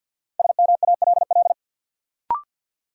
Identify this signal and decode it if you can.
Morse code, heard as a keyed tone, about 52 wpm, 700 Hz. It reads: SMALL
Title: Sentences from Top 200 Words - New Word Drill - 50wpm